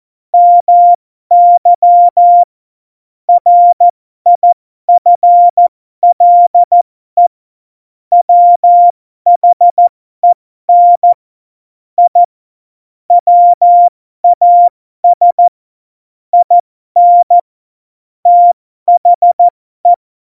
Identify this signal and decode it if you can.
Morse code, heard as a keyed tone, about 14 wpm, spaced as for 12 wpm, 700 Hz. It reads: MY RIFLE WHEN I WAS IN THE
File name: Morsekurs / Witze